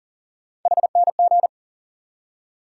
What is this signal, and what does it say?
Morse code, heard as a keyed tone, about 40 wpm, 700 Hz. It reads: HNG
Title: USA Call Sign Postfixes - 3 Letters 40wpm